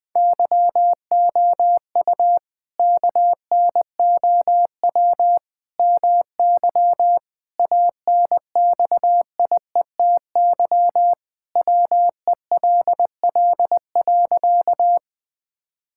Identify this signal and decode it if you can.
Morse code, heard as a keyed tone, about 20 wpm, 700 Hz. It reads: YOU KNOW MY ANXIETY WELL.
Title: Sentences from Top 1600 Words 20wpm